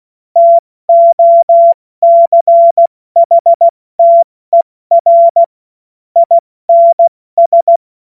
Morse code, heard as a keyed tone, about 16 wpm, 680 Hz. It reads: TOCHTER INS
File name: Morsekurs / Witze / Witz 14. März 2023 70 F80